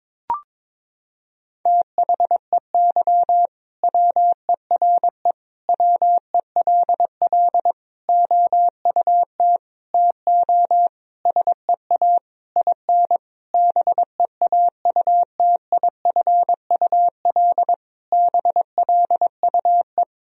Morse code, heard as a keyed tone, about 22 wpm, 700 Hz. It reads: THEY WERE WELL OUT TO SEA IN BEAUTIFUL BLUE
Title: Sentences from Top 1000 Words - Review Part 3 - 22wpm